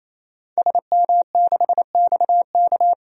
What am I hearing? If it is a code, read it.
Morse code, heard as a keyed tone, about 28 wpm, 700 Hz. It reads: SM6XK
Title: DXCC-100 Call Signs - 5 Characters - 28wpm